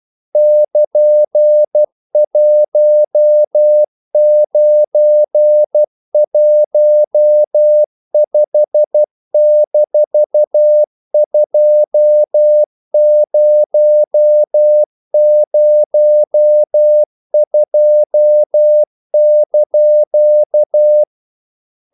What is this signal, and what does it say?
Morse code, heard as a keyed tone, about 12 wpm, 600 Hz. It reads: <KN>1915-2002)
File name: CW / Practice / EnglishQuotations0112